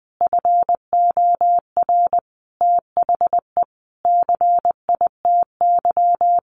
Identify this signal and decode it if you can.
Morse code, heard as a keyed tone, about 20 wpm, 700 Hz. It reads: FOR THE CITY
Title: Sentences from Top 1400 Words 20wpm